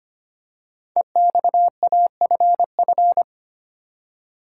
Morse code, heard as a keyed tone, about 25 wpm, 700 Hz. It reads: EXAFF
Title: Code Groups - 5 Letters - 25wpm